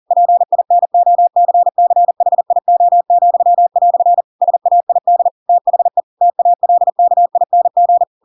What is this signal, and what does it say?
Morse code, heard as a keyed tone, about 40 wpm, 700 Hz. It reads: PINOCCHIO," SAID THE TALKING
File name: Pinocchio / Pinocchio ch4 40 40 ik1hge